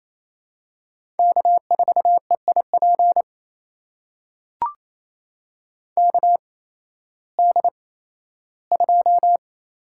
Morse code, heard as a keyed tone, about 28 wpm, spaced as for 15 wpm, 700 Hz. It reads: K4EIP KD2
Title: USA Call Signs - 8x Character Spacing - 28wpm